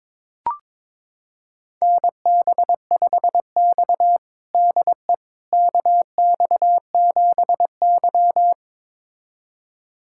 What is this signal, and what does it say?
Morse code, heard as a keyed tone, about 22 wpm, 700 Hz. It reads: NB5X DE KX7Y
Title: USA Callsign DE Callsign - 4 Characters - 22wpm